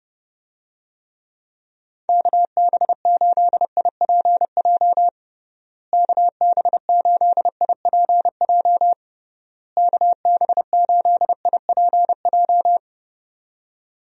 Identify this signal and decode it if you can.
Morse code, heard as a keyed tone, about 30 wpm, 700 Hz. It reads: KB8IPJ KB8IPJ KB8IPJ
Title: USA Call Signs - 6 Characters - Repeated 3x - 30wpm